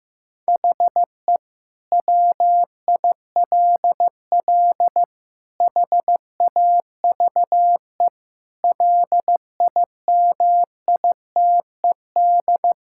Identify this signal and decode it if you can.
Morse code, heard as a keyed tone, about 15 wpm, 700 Hz. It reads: HE WILL HAVE LIMITED
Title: Sentences from Top 1000 Words - Review Part 6 - 15wpm